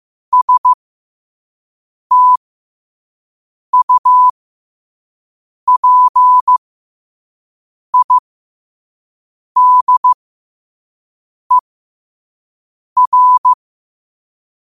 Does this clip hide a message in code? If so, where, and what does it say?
Yes: Morse code from 0.3 to 13.5 s, heard as a keyed tone, about 15 wpm, spaced as for 5 wpm, 1000 Hz. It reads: STUPIDER